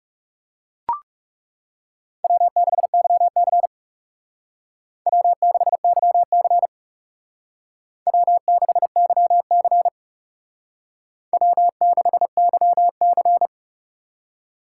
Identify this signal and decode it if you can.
Morse code, heard as a keyed tone, about 38 wpm, 700 Hz. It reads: W6YC W6YC W6YC W6YC